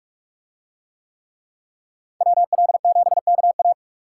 Morse code, heard as a keyed tone, about 43 wpm, 700 Hz. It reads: WL7KA